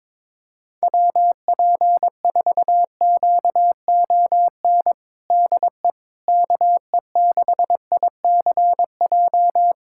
Morse code, heard as a keyed tone, about 21 wpm, 700 Hz. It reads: WP4QON DE KE6ICJ
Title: USA Callsign DE Callsign 22wpm